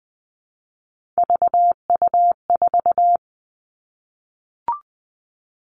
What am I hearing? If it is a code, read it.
Morse code, heard as a keyed tone, about 20 wpm, 700 Hz. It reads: VU4
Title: All Cut Numbers - 3-Digit 20wpm